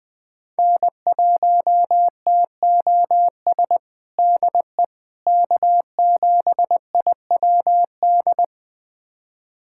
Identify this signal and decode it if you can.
Morse code, heard as a keyed tone, about 21 wpm, 700 Hz. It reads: N1TOS DE K7IWD
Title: USA Callsign DE Callsign - 5 Characters - 20wpm